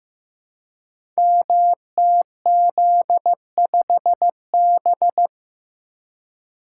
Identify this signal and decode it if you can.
Morse code, heard as a keyed tone, about 15 wpm, 700 Hz. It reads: MTZ5B
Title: Code Groups - 5 Letter:Numbers - 15wpm